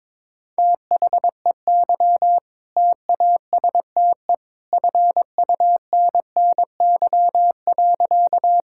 Morse code, heard as a keyed tone, about 22 wpm, 700 Hz. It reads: THEY TASTE FUNNY.
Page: morse-code-ninja-practice-22wpm